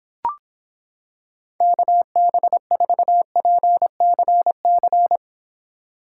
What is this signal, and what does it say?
Morse code, heard as a keyed tone, about 26 wpm, 700 Hz. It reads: KB4PCC